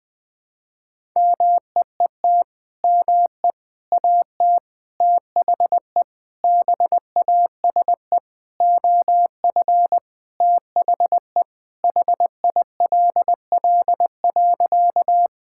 Morse code, heard as a keyed tone, about 20 wpm, 700 Hz. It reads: MEET ME AT THE BASE OF THE HILL.
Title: Sentences with 5-Word Phrases - Spoken Beginning - 20wpm